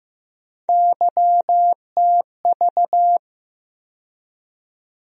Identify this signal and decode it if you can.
Morse code, heard as a keyed tone, about 15 wpm, 700 Hz. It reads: YTV
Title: USA Call Sign Postfixes - 3 Letters 15wpm